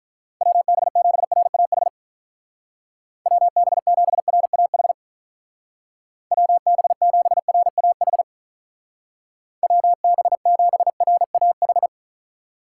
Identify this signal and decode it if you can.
Morse code, heard as a keyed tone, about 44 wpm, 700 Hz. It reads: WB7RAH WB7RAH WB7RAH WB7RAH
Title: Speed Racing - USA 2x3 Call Signs 35wpm